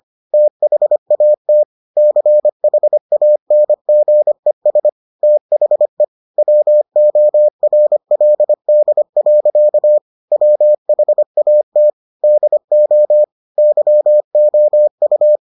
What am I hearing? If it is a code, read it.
Morse code, heard as a keyed tone, about 25 wpm, 600 Hz. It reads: THAT CHANGES THE WORLD. WHAT DO YOU